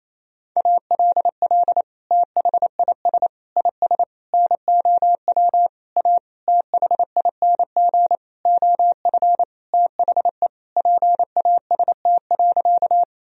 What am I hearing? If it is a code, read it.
Morse code, heard as a keyed tone, about 28 wpm, 700 Hz. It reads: ALL THIS IS NOW A THING OF THE PAST.
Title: Sentences from Top 1000 Words - Review Part 3 - 28wpm